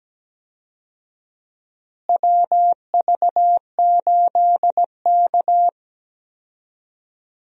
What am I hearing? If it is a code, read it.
Morse code, heard as a keyed tone, about 17 wpm, 700 Hz. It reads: WV8K
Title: USA Call Signs - 4 Characters - Repeated 3x - 17wpm